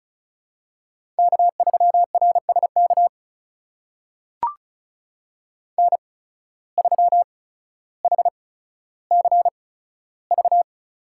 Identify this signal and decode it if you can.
Morse code, heard as a keyed tone, about 36 wpm, spaced as for 15 wpm, 700 Hz. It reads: K3RSK N3HCV
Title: USA Call Signs - 5 Characters - 8x Character Spacing - 35wpm